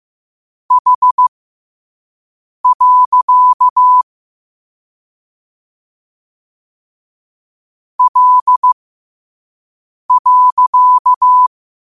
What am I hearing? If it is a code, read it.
Morse code, heard as a keyed tone, about 15 wpm, spaced as for 5 wpm, 1000 Hz. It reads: H. L.
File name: 5wpm